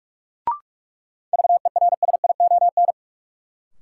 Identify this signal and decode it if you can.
Morse code, heard as a keyed tone, about 46 wpm, 700 Hz. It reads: VERSION
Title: Sentences from Top 1400 Words - New Word Drill - 45wpm